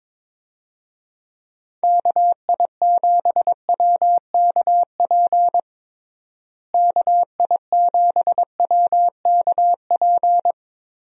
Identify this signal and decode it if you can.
Morse code, heard as a keyed tone, about 22 wpm, 700 Hz. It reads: KI7WKP KI7WKP
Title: USA Call Signs - 6 Characters - Repeated 2x - 22wpm